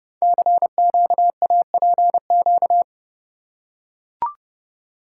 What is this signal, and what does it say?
Morse code, heard as a keyed tone, about 28 wpm, 700 Hz. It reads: CQAPQ
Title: Code Groups - 5 Letters - 30wpm